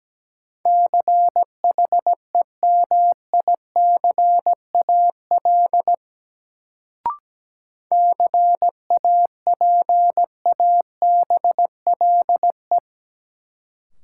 Morse code, heard as a keyed tone, about 17 wpm, 700 Hz. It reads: CHEMICAL CAPABLE
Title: Sentences from Top 1200 Words - New Word Drill Alt - 17wpm